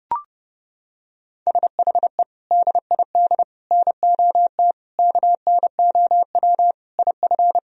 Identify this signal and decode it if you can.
Morse code, heard as a keyed tone, about 32 wpm, 700 Hz. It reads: SHE DID NOT KNOW IF
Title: Sentences from Top 1000 Words - Review Part 4 - 30wpm